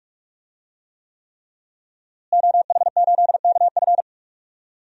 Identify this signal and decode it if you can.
Morse code, heard as a keyed tone, about 46 wpm, 700 Hz. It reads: OH8KF
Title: DXCC-100 Call Signs - 5 Characters - Repeated 3x - 45wpm